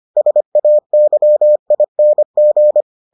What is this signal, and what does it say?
Morse code, heard as a keyed tone, about 25 wpm, 600 Hz. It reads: SAYING